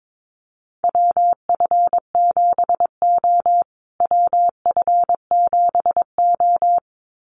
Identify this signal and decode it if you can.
Morse code, heard as a keyed tone, about 22 wpm, 700 Hz. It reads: WF7O WF7O
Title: USA Call Signs Repeated 2x 22wpm